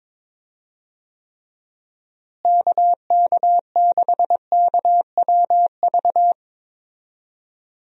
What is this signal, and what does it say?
Morse code, heard as a keyed tone, about 21 wpm, 700 Hz. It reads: KK6KWV
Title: USA Call Signs - 6 Characters - Repeated 3x - 22wpm